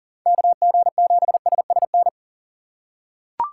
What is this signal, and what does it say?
Morse code, heard as a keyed tone, about 40 wpm, 700 Hz. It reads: KG7SSN
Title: USA Call Signs - 6 Characters - 4x Character Spacing - 40wpm